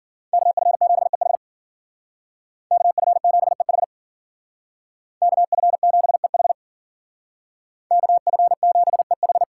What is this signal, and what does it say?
Morse code, heard as a keyed tone, about 52 wpm, 700 Hz. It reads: KF7EH KF7EH KF7EH KF7EH